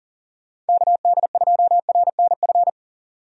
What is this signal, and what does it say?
Morse code, heard as a keyed tone, about 40 wpm, 700 Hz. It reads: KD2RNF
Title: USA Call Signs - 6 Characters - 40wpm